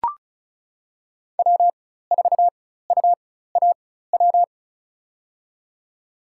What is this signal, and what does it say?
Morse code, heard as a keyed tone, about 36 wpm, spaced as for 16 wpm, 700 Hz. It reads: W4UAW
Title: USA Call Signs - 4x Character Spacing - 35wpm